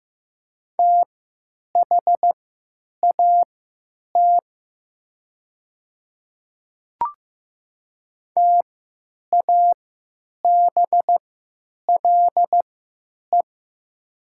Morse code, heard as a keyed tone, about 15 wpm, spaced as for 9 wpm, 700 Hz. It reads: THAT TABLE
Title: 3x Character Spacing - Top 500 Words - 15wpm